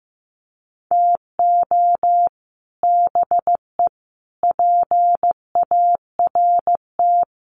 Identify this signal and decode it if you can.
Morse code, heard as a keyed tone, about 15 wpm, 700 Hz. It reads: TO BE PART
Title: Sets of 4 Words 15wpm